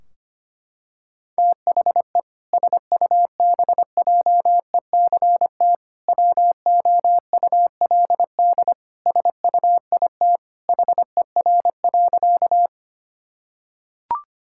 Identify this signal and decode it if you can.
Morse code, heard as a keyed tone, about 25 wpm, 700 Hz. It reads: THE SUBJECT WOULD SUIT HER.
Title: Sentences from Top 1600 Words 25wpm